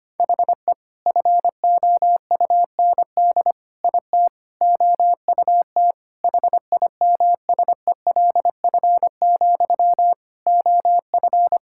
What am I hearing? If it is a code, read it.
Morse code, heard as a keyed tone, about 25 wpm, 700 Hz. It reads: HE FOUND IT OUT HIMSELF, OF
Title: Sentences from Top 1600 Words 25wpm